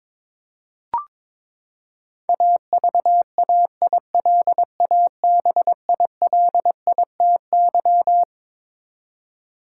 Morse code, heard as a keyed tone, about 22 wpm, 700 Hz. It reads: AVAILABILITY
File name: Twelve Letter Words - 22wpm